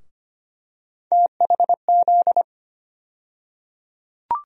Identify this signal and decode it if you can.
Morse code, heard as a keyed tone, about 25 wpm, 700 Hz. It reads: THZ
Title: USA Call Sign Postfixes - 3 Letters 25wpm